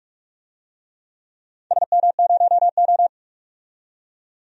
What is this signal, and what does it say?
Morse code, heard as a keyed tone, about 42 wpm, 700 Hz. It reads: SM0O
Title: DXCC-100 Call Signs - 4 Characters - 45wpm